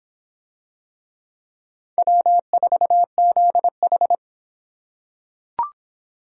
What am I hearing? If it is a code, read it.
Morse code, heard as a keyed tone, about 24 wpm, 700 Hz. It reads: W4ZH